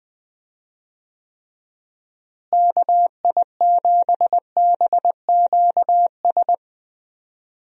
Morse code, heard as a keyed tone, about 20 wpm, 700 Hz. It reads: KI7BQS
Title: USA Call Signs - 6 Characters - Repeated 3x - 20wpm